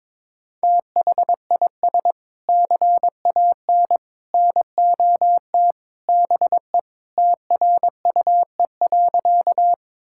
Morse code, heard as a keyed tone, about 22 wpm, 700 Hz. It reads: THIS CAN NOT BE TRUE.
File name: practice / 22wpm / Sentences with 3-Word Phrases - 22wpm